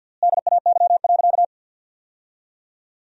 Morse code, heard as a keyed tone, about 49 wpm, 700 Hz. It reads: DAY.
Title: 4x Word-Spacing - Sentences from Top 200 Words - 50wpm